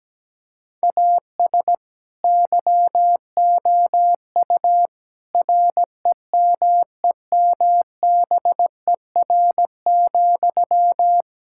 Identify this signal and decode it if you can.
Morse code, heard as a keyed tone, about 17 wpm, 700 Hz. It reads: AS YOU REMEMBER,
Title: Sentences from Top 1000 Words - Review Part 5 - 17wpm